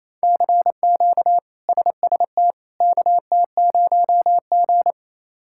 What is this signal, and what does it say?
Morse code, heard as a keyed tone, about 28 wpm, 700 Hz. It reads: CQ SST KT0G
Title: K1USN SST QSO Practice - 28wpm